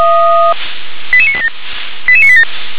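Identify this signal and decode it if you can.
Five-tone selective call (ZVEI), heard as quick runs of five tones, tones of about 70 ms. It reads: 88987 88987
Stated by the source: Five-tone ZVEI selective call 88972, received by SDR.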